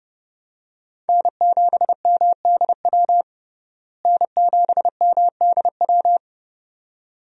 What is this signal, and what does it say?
Morse code, heard as a keyed tone, about 30 wpm, 700 Hz. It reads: N7MDW N7MDW
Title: USA Call Signs - 5 Characters - Repeated 2x - 30wpm